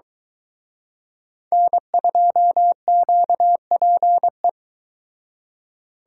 Morse code, heard as a keyed tone, about 23 wpm, 700 Hz. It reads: N2QPE